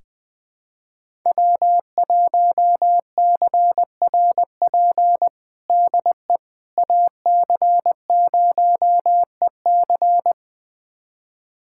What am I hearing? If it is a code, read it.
Morse code, heard as a keyed tone, about 20 wpm, 700 Hz. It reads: W1CRP DE AC0EC
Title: USA Callsign DE Callsign - 5 Characters - 20wpm